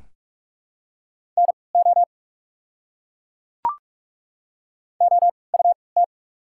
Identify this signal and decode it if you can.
Morse code, heard as a keyed tone, about 47 wpm, 690 Hz. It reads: N O O U T
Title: 3x Character Spacing - Top 200 Words - 45wpm